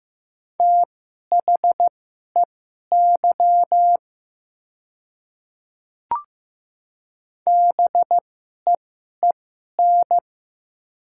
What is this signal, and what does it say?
Morse code, heard as a keyed tone, about 15 wpm, spaced as for 11 wpm, 700 Hz. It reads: THEY BEEN
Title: 2x Character Spacing - Top 100 Words - 15wpm